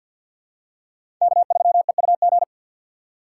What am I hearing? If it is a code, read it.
Morse code, heard as a keyed tone, about 52 wpm, 700 Hz. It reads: K3EUG
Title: USA Call Signs - 2x Character Spacing - 50wpm